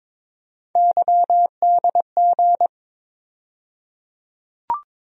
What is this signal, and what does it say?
Morse code, heard as a keyed tone, about 21 wpm, 700 Hz. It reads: YDG
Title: USA Call Sign Postfixes - 3 Letters 22wpm